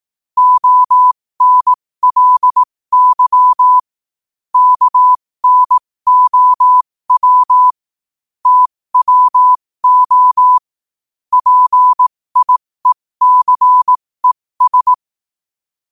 Morse code, heard as a keyed tone, about 18 wpm, spaced as for 15 wpm, 1000 Hz. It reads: ONLY KNOW TWO PIECES
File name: qotdfiles / 15wpm